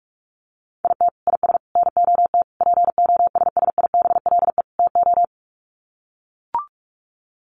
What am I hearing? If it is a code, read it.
Morse code, heard as a keyed tone, about 44 wpm, 700 Hz. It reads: IT IS NOT POSSIBLE TO
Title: Sets of 5 Words Encore2 45wpm